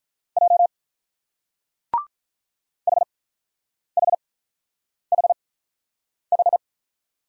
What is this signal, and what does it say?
Morse code, heard as a keyed tone, about 44 wpm, spaced as for 9 wpm, 700 Hz. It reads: J HHHH